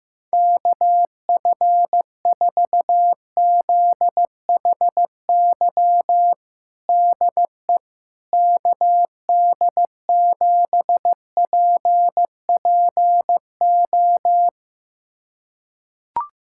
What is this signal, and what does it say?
Morse code, heard as a keyed tone, about 16 wpm, 700 Hz. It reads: KF4ZHY DE KD7PPO
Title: USA Callsign DE Callsign 15wpm